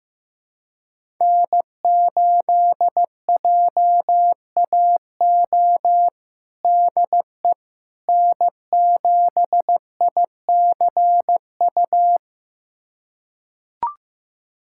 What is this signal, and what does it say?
Morse code, heard as a keyed tone, about 15 wpm, 700 Hz. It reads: N8JAO DE N7ICU
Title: USA Callsign DE Callsign 15wpm